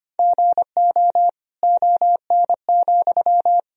Morse code, heard as a keyed tone, about 25 wpm, 700 Hz. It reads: GO ON,
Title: Sentences from Top 1000 Words - Review Part 4 - 25wpm